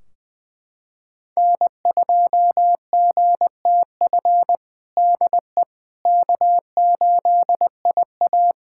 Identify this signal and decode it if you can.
Morse code, heard as a keyed tone, about 20 wpm, 700 Hz. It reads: N2GTF DE K8IA
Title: Practice QSO - Exchange 4 - Ending - 20wpm